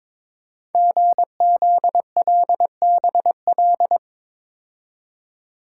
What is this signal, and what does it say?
Morse code, heard as a keyed tone, about 22 wpm, 700 Hz. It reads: GZLBL
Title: Code Groups - 5 Letters - 22wpm